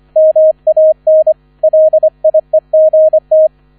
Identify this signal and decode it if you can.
Morse code, heard as a keyed tone, about 25 wpm, 630 Hz. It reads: MAN LIEGT